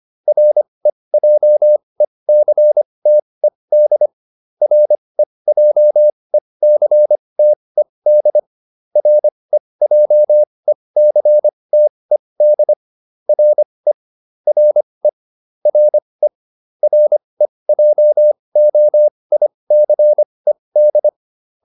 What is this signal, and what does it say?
Morse code, heard as a keyed tone, about 25 wpm, spaced as for 20 wpm, 600 Hz. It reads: REJECTED REJECTED REJECTED RE RE RE REJOICED